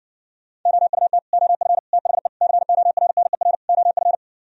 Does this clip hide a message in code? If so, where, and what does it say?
Yes: Morse code from 0.7 to 4.2 s, heard as a keyed tone, about 60 wpm, 700 Hz. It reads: OUT OF THE CORNER OF